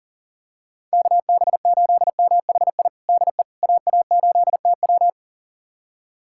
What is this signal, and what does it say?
Morse code, heard as a keyed tone, about 37 wpm, 700 Hz. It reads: KB8MHI DE AA8TW
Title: USA Callsign DE Callsign 40wpm